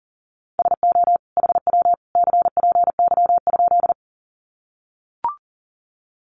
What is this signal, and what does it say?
Morse code, heard as a keyed tone, about 40 wpm, 700 Hz. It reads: SO HW CPY?
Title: Common QSO Phrases 40wpm